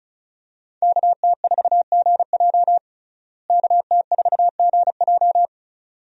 Morse code, heard as a keyed tone, about 34 wpm, 700 Hz. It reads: KT4GJ KT4GJ
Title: USA Call Signs - 5 Characters - Repeated 2x - 35wpm